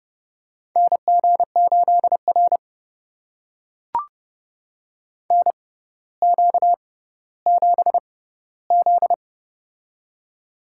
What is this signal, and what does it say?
Morse code, heard as a keyed tone, about 30 wpm, spaced as for 15 wpm, 700 Hz. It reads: NG8R NQ7Z
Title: USA Call Signs - 4 Characters - 6x Character Spacing - 30wpm